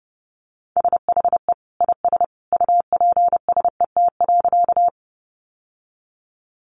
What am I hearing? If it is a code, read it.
Morse code, heard as a keyed tone, about 30 wpm, 700 Hz. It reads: SHE IS UPSET.
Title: Sentences from Top 800 Words 30wpm